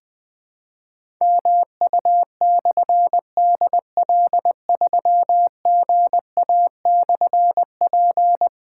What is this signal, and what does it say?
Morse code, heard as a keyed tone, about 20 wpm, 700 Hz. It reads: MU/DL3GA/P
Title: Difficult Call Signs 20wpm